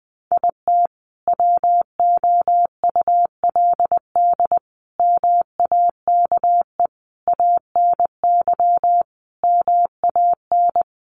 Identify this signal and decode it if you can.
Morse code, heard as a keyed tone, about 20 wpm, 700 Hz. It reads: IT WOULD MAKE ANY MAN
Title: Sentences from Top 1100 Words 20wpm